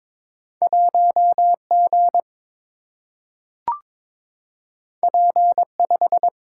Morse code, heard as a keyed tone, about 22 wpm, 700 Hz. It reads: 1G P5
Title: Single-Letter Single-Number 22wpm